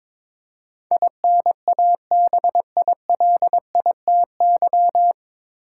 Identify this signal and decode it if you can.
Morse code, heard as a keyed tone, about 22 wpm, 700 Hz. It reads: INABILITY
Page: Morse Code Ninja practice 22wpm